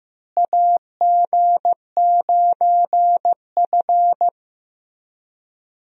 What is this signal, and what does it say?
Morse code, heard as a keyed tone, about 15 wpm, 700 Hz. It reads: AG9F
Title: USA Call Signs - 4 Characters - Repeated 3x - 15wpm